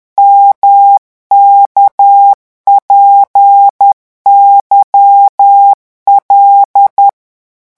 Morse code, heard as a keyed tone, about 10 wpm, 800 Hz. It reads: MKPYL
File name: morse mp3 / revision11 lettres vitesse 10